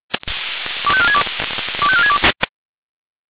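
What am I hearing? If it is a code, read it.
Five-tone selective call (ZVEI), heard as quick runs of five tones, tones of about 70 ms. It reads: 25462 25462
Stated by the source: ZVEI five-tone selective call 85267 received by SDR.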